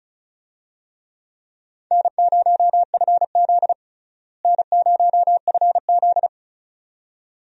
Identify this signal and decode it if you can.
Morse code, heard as a keyed tone, about 36 wpm, 700 Hz. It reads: N0FZ N0FZ
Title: USA Call Signs - 4 Characters - Repeated 2x - 35wpm